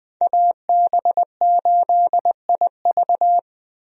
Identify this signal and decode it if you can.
Morse code, heard as a keyed tone, about 20 wpm, 700 Hz. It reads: AB8IV
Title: USA Call Signs - 5 Characters - Repeated 3x - 20wpm